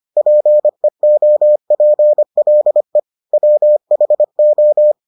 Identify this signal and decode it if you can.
Morse code, heard as a keyed tone, about 25 wpm, 600 Hz. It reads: PEOPLE WHO